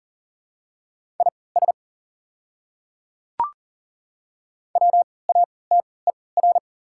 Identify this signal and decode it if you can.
Morse code, heard as a keyed tone, about 37 wpm, spaced as for 22 wpm, 700 Hz. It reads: IS WATER
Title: 3x Character Spacing - Top 100 Words - 40wpm